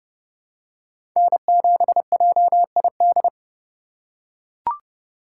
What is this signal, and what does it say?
Morse code, heard as a keyed tone, about 32 wpm, 700 Hz. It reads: N7JID
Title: USA Call Signs - 5 Characters - 8x Character Spacing - 30wpm